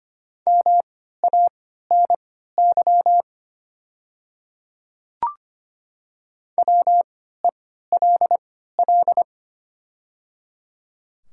Morse code, heard as a keyed tone, about 25 wpm, spaced as for 14 wpm, 700 Hz. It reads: MANY WELL
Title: 3x Character Spacing - Top 300 Words - 25wpm